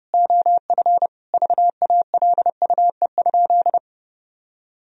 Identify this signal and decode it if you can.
Morse code, heard as a keyed tone, about 28 wpm, 700 Hz. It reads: OF VALUE?
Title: Sentences from Top 1000 Words - Review Part 3 - 30wpm